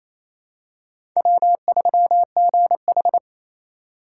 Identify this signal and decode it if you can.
Morse code, heard as a keyed tone, about 28 wpm, 700 Hz. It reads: W3GH
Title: USA Call Signs - 4 Characters - 4x Character Spacing - 28wpm